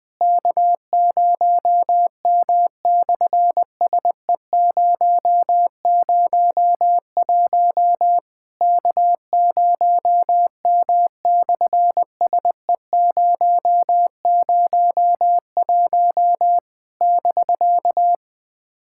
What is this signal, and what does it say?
Morse code, heard as a keyed tone, about 20 wpm, 700 Hz. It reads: K0M/SE001 K0M/SE001 <BK>
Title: SOTA QSO Practice 20wpm